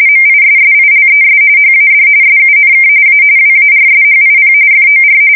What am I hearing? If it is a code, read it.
RTTY, about 45.45 baud. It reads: MISSILES HAD SUNK A WARSHIP FOR
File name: fsk170 / battleshipnj